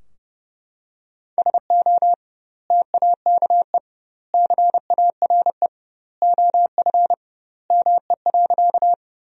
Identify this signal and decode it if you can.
Morse code, heard as a keyed tone, about 32 wpm, 700 Hz. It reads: SO TAKE CARE OF ME.
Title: 2x Word-Spacing - Sentences from Top 300 Words - 30wpm